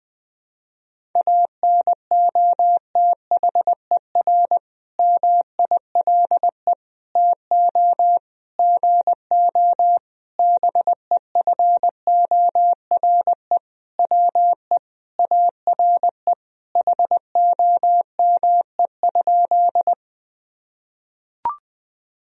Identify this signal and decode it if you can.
Morse code, heard as a keyed tone, about 20 wpm, 700 Hz. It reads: ANOTHER MILE TO GO BEFORE WE ARE HOME?
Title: Sentences from Top 1000 Words - Review Part 1 - 20wpm